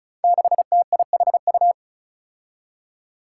Morse code, heard as a keyed tone, about 36 wpm, 700 Hz. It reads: 6TIHU
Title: Five Challenging Letter-Number Combinations 35wpm